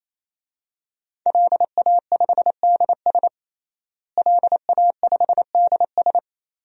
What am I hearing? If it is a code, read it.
Morse code, heard as a keyed tone, about 27 wpm, 700 Hz. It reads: LA5DS LA5DS
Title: DXCC-100 Call Signs - 5 Characters - Repeated 3x - 28wpm